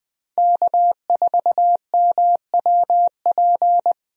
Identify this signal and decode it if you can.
Morse code, heard as a keyed tone, about 20 wpm, 700 Hz. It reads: K4MWP
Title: USA Call Signs - 5 Characters - Repeated 3x - 20wpm